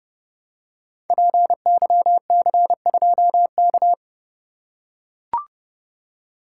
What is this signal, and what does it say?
Morse code, heard as a keyed tone, about 30 wpm, 700 Hz. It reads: PYC2K
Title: Code Groups - 5 Letter:Numbers - 30wpm